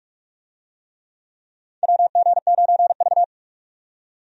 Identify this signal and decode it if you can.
Morse code, heard as a keyed tone, about 45 wpm, 700 Hz. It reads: WG9V
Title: USA Call Signs - 4 Characters - 45wpm